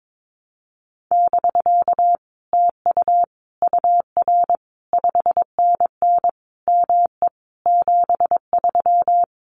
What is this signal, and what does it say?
Morse code, heard as a keyed tone, about 22 wpm, 700 Hz. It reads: <BK> TU UR 5NN ME 73
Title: Common QSO Phrases Encore 22wpm